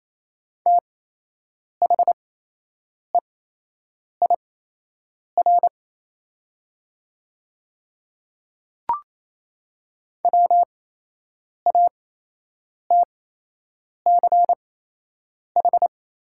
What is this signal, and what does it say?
Morse code, heard as a keyed tone, about 28 wpm, spaced as for 8 wpm, 700 Hz. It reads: THEIR WATCH
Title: 8x Character Spacing - Top 300 Words - 28wpm